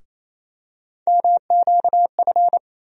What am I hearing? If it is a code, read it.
Morse code, heard as a keyed tone, about 28 wpm, 700 Hz. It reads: MQF
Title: USA Call Sign Postfixes - 3 Letters 28wpm